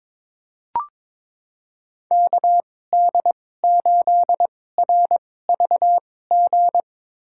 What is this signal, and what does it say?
Morse code, heard as a keyed tone, about 22 wpm, spaced as for 16 wpm, 700 Hz. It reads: KD8RVG
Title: USA Call Signs - 6 Characters - 2x Character Spacing - 22wpm